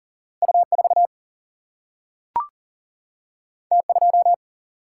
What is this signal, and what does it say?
Morse code, heard as a keyed tone, about 40 wpm, 700 Hz. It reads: U 4 T 2
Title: Single-Letter Single-Number 40wpm